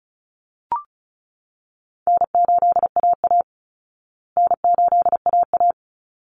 Morse code, heard as a keyed tone, about 34 wpm, 700 Hz. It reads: N8AA N8AA
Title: Easy Call Signs Repeated 2x - 35wpm